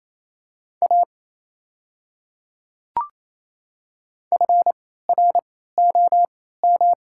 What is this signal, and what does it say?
Morse code, heard as a keyed tone, about 27 wpm, spaced as for 16 wpm, 690 Hz. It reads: A FROM
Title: 3x Character Spacing - Top 100 Words - 28wpm